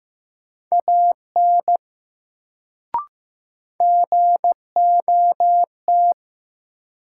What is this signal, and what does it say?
Morse code, heard as a keyed tone, about 16 wpm, 700 Hz. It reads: AN GOT
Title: Sentences from Top 300 Words - New Word Drill - 15wpm